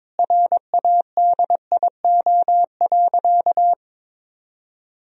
Morse code, heard as a keyed tone, about 22 wpm, 700 Hz. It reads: RADIO.
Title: Sentences from Top 1000 Words - Review Part 3 - 22wpm